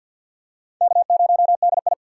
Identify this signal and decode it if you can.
Morse code, heard as a keyed tone, about 51 wpm, 700 Hz. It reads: K0DI
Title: USA Call Signs - 3x Character Spacing - 50wpm